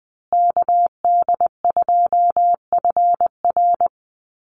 Morse code, heard as a keyed tone, about 20 wpm, 700 Hz. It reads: KD2FR
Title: USA Call Signs Repeated 2x 20wpm